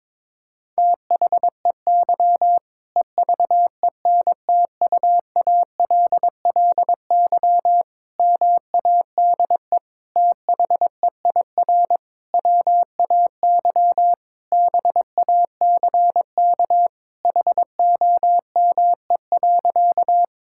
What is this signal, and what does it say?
Morse code, heard as a keyed tone, about 22 wpm, 700 Hz. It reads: THEY EVENTUALLY MADE THEIR WAY BACK HOME.
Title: Sentences from Top 1000 Words - Review Part 6 - 22wpm